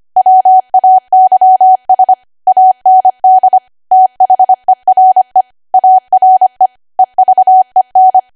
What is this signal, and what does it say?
Morse code, heard as a keyed tone, about 25 wpm, 750 Hz. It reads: WAYS AND THERE ARE EVEN
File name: W1AW-Level3/140218 25WPM